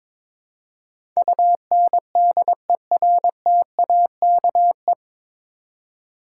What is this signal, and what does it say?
Morse code, heard as a keyed tone, about 22 wpm, 700 Hz. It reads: UNDERTAKE
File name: 9 Letter Words - First 3 Letters Given - 22wpm